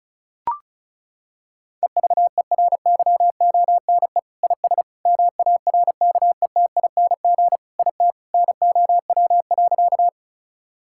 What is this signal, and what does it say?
Morse code, heard as a keyed tone, about 36 wpm, 690 Hz. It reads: EVERYONE IS MARKETING IT NOW.
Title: Sentences from Top 1000 Words - Review Part 5 - 35wpm